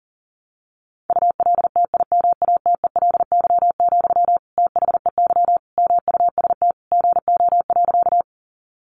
Morse code, heard as a keyed tone, about 40 wpm, 700 Hz. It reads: ULTIMATELY, THEY MUST GO.
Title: Sentences from Top 1200 Words 40wpm